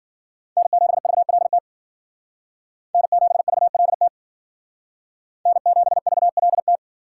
Morse code, heard as a keyed tone, about 51 wpm, 700 Hz. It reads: N7VLT N7VLT N7VLT